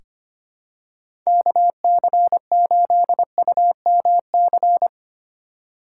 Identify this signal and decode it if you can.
Morse code, heard as a keyed tone, about 25 wpm, 700 Hz. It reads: KC8UMC
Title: USA Call Signs - 6 Characters - 4x Character Spacing - 25wpm